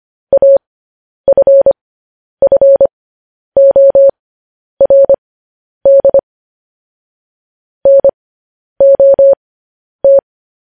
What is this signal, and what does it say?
Morse code, heard as a keyed tone, about 25 wpm, spaced as for 10 wpm, 550 Hz. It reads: AFFORD NOT
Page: news headlines in morse code at 10 wpm